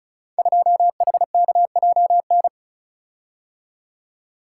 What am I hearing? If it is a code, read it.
Morse code, heard as a keyed tone, about 36 wpm, 700 Hz. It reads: 2HKJN
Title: Code Groups - 5 Letter:Numbers - 35wpm